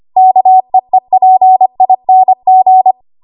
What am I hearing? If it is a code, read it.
Morse code, heard as a keyed tone, about 25 wpm, 750 Hz. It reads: KEEPING
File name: Archive / 25 WPM / 25WPM